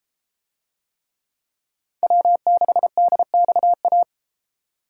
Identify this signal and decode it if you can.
Morse code, heard as a keyed tone, about 31 wpm, 700 Hz. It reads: W6DXA